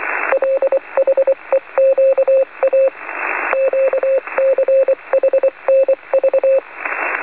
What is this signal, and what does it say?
Morse code, heard as a keyed tone, about 24 wpm, 540 Hz. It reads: LHEQA QCHNV
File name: 4.440-CW-in-USB-niet-mee-te-lezen-15-06-2012-1